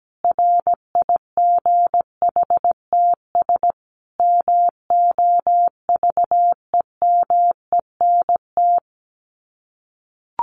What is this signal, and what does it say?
Morse code, heard as a keyed tone, about 17 wpm, 700 Hz. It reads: RIGHTS MOVEMENT
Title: Sets of 5 Words Encore2 17wpm